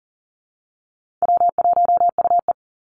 Morse code, heard as a keyed tone, about 40 wpm, 700 Hz. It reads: W1UE
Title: Easy Call Signs Repeated 2x - 40wpm